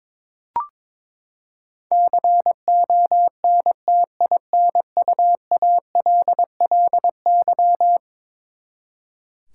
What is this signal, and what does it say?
Morse code, heard as a keyed tone, about 22 wpm, 700 Hz. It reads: CONTINUALLY
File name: practice / 22wpm / Eleven Letter Words - 22wpm